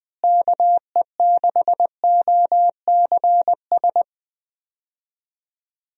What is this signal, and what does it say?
Morse code, heard as a keyed tone, about 21 wpm, 700 Hz. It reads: KE6OCS
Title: USA Call Signs - 6 Characters - Repeated 3x - 20wpm